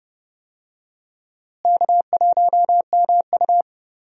Morse code, heard as a keyed tone, about 31 wpm, 700 Hz. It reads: K1MU
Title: USA Call Signs - 4 Characters - Repeated 3x - 30wpm